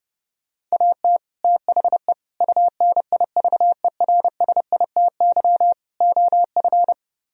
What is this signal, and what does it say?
Morse code, heard as a keyed tone, about 38 wpm, 700 Hz. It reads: AT THE UNIVERSITY OF